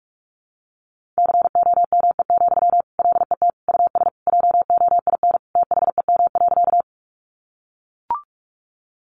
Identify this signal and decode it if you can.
Morse code, heard as a keyed tone, about 44 wpm, 700 Hz. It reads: COME, LET US JOIN THEM.
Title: Sentences from Top 1400 Words 45wpm